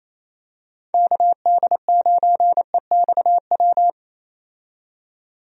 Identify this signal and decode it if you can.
Morse code, heard as a keyed tone, about 28 wpm, 700 Hz. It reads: KD9EXW
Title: USA Call Signs - 6 Characters - 28wpm